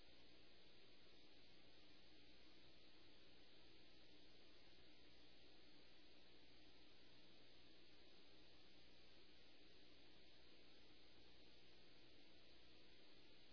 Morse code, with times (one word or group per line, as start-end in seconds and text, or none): none